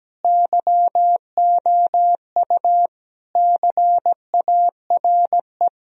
Morse code, heard as a keyed tone, about 17 wpm, 700 Hz. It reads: YOU CARE
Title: Sentences from Top 1000 Words - Review Part 1 - 17wpm